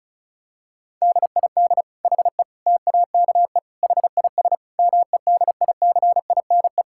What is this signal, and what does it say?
Morse code, heard as a keyed tone, about 36 wpm, 700 Hz. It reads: DID HE TAKE HIS MEDICINE
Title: Sentences from Top 1000 Words - Review Part 6 - 35wpm